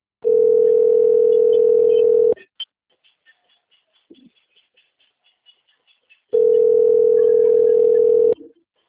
Ringback tone (North America), 440 + 480 Hz.